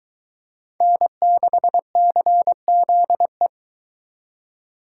Morse code, heard as a keyed tone, about 23 wpm, 700 Hz. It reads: N6CZE